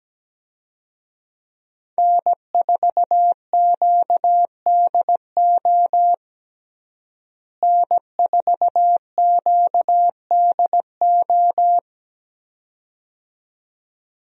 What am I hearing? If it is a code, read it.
Morse code, heard as a keyed tone, about 17 wpm, 700 Hz. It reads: N4QDO N4QDO
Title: USA Call Signs - 5 Characters - Repeated 2x - 17wpm